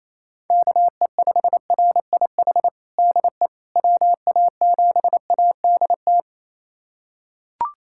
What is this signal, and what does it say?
Morse code, heard as a keyed tone, about 28 wpm, 700 Hz. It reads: KE5RIH DE WA7ADT
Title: USA Callsign DE Callsign - 6 Characters - 28wpm